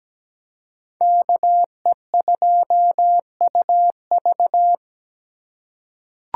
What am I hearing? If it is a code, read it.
Morse code, heard as a keyed tone, about 17 wpm, 700 Hz. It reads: KE2UV
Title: USA Call Signs - 5 Characters - 2x Character Spacing - 17wpm